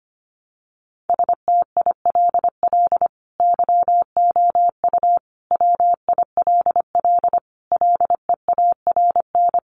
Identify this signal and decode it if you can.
Morse code, heard as a keyed tone, about 25 wpm, 700 Hz. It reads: STILL YOU WILL LEARN
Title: Sentences from Top 600 Words 25wpm